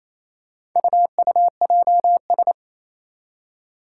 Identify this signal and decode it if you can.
Morse code, heard as a keyed tone, about 27 wpm, 700 Hz. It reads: UUJS
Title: Four Challenging Letter-Number Combinations 28wpm